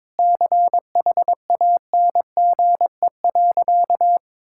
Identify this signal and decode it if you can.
Morse code, heard as a keyed tone, about 21 wpm, 690 Hz. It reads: CHANGE.
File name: practice / 22wpm / Easy Sentences with Hints - 22wpm